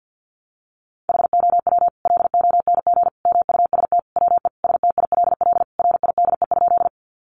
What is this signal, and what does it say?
Morse code, heard as a keyed tone, about 50 wpm, 700 Hz. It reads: HOW LONG MUST WE STILL RIDE?
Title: Sentences from Top 1200 Words 50wpm